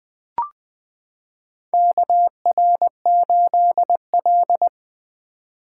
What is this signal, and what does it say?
Morse code, heard as a keyed tone, about 20 wpm, 700 Hz. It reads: KR8L
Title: K1USN SST QSO Practice - 20wpm